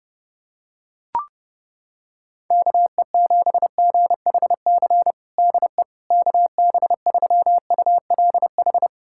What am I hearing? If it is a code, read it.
Morse code, heard as a keyed tone, about 30 wpm, 700 Hz. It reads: KE7GHC DE KB3ULH
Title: USA Callsign DE Callsign - 6 Characters - 30wpm